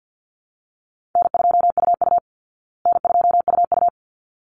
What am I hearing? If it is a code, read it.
Morse code, heard as a keyed tone, about 50 wpm, 700 Hz. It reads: N2UU N2UU
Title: Easy Call Signs Repeated 2x - 50wpm